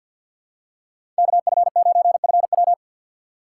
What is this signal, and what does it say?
Morse code, heard as a keyed tone, about 51 wpm, 700 Hz. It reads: KF9FW
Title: USA Call Signs - 4x Character Spacing - 50wpm